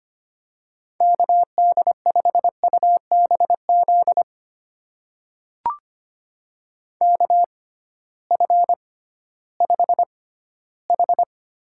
Morse code, heard as a keyed tone, about 25 wpm, spaced as for 15 wpm, 700 Hz. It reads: KD5UBZ KF5H
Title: USA Call Signs - 6 Characters - 6x Character Spacing - 25wpm